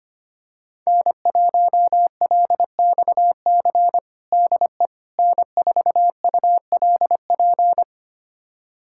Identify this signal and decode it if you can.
Morse code, heard as a keyed tone, about 25 wpm, 700 Hz. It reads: N1LXC DE N4ULP
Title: USA Callsign DE Callsign - 5 Characters - 25wpm